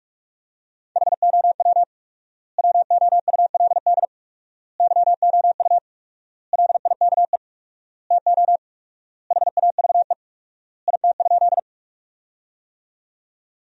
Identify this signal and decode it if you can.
Morse code, heard as a keyed tone, about 42 wpm, 700 Hz. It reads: HOW WOULD YOU LIKE TO HAVE IT?
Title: 4x Word-Spacing - Sentences from Top 100 Words - 45wpm